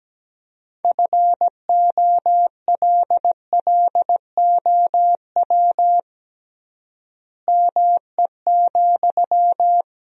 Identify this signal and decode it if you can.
Morse code, heard as a keyed tone, about 17 wpm, 700 Hz. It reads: FOLLOW ME,
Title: 3x Word-Spacing - Sentences from Top 200 Words - 17wpm